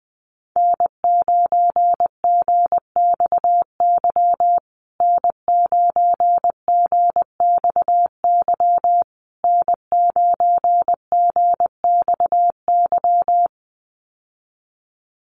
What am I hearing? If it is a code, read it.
Morse code, heard as a keyed tone, about 20 wpm, 700 Hz. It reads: N9GXY N9GXY N9GXY
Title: USA Call Signs Repeated 3x 20wpm